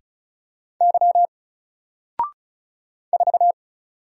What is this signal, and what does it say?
Morse code, heard as a keyed tone, about 35 wpm, 700 Hz. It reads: Y 4
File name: Single-Letter-Number-35wpm